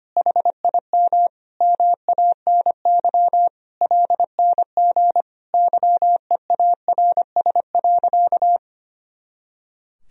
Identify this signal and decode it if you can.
Morse code, heard as a keyed tone, about 25 wpm, 700 Hz. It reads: HIM MANY LNG YEARS.
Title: Sentences from Top 200 Words and Abbreviations 25wpm